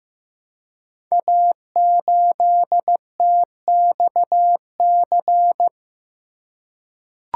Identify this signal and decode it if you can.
Morse code, heard as a keyed tone, about 15 wpm, 700 Hz. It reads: A8TXC
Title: Code Groups - 5 Letter:Numbers - 15wpm